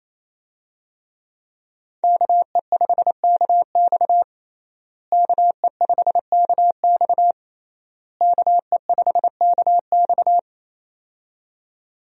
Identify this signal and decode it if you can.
Morse code, heard as a keyed tone, about 28 wpm, 700 Hz. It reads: KE5KX KE5KX KE5KX
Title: USA Call Signs - 5 Characters - Repeated 3x - 28wpm